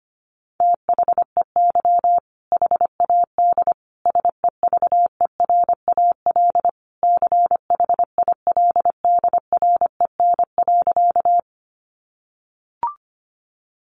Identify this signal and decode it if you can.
Morse code, heard as a keyed tone, about 25 wpm, 700 Hz. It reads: THEY HAD SEVERAL CHILDREN.
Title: Sentences from Top 400 Words 25wpm